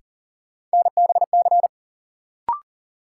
Morse code, heard as a keyed tone, about 40 wpm, 700 Hz. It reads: NBC
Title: Sets of 5 Words Encore - New Word Drill - 40wpm